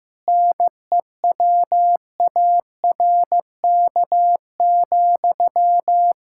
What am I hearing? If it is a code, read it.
Morse code, heard as a keyed tone, about 15 wpm, 700 Hz. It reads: NEWARK,
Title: Practice QSO - Exchange 1 - Beginning - 15wpm